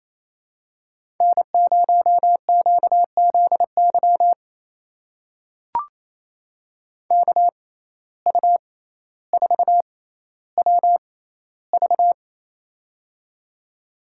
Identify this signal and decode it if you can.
Morse code, heard as a keyed tone, about 28 wpm, spaced as for 14 wpm, 700 Hz. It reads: N0QZY KU4WV
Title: USA Call Signs - 5 Characters - 6x Character Spacing - 28wpm